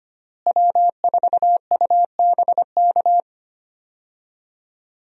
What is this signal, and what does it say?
Morse code, heard as a keyed tone, about 25 wpm, 700 Hz. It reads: W4UBK
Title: USA Call Signs - 5 Characters - Repeated 3x - 25wpm